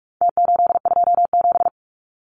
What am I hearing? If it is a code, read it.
Morse code, heard as a keyed tone, about 44 wpm, 700 Hz. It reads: T827
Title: Numbers - Contest Style - 4-Digit 45wpm